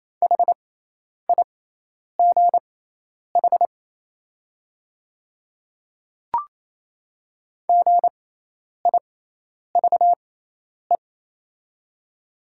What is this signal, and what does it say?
Morse code, heard as a keyed tone, about 27 wpm, spaced as for 10 wpm, 700 Hz. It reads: HIGH GIVE
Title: 6x Character Spacing - Top 200 Words - 28wpm